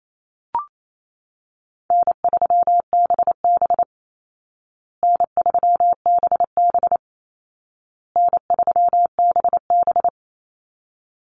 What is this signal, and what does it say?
Morse code, heard as a keyed tone, about 28 wpm, 700 Hz. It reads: N3BB N3BB N3BB
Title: Easy Call Signs Repeated 3x - 28wpm